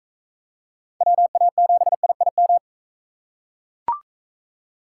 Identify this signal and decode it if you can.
Morse code, heard as a keyed tone, about 40 wpm, 700 Hz. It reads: WA7IIM